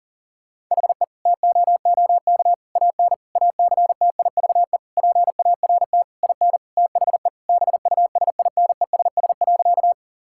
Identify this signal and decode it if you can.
Morse code, heard as a keyed tone, about 40 wpm, 700 Hz. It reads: HE TOOK AN ACTIVE PART IN THE BUSINESS.
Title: Sentences from Top 1000 Words - Review Part 6 - 40wpm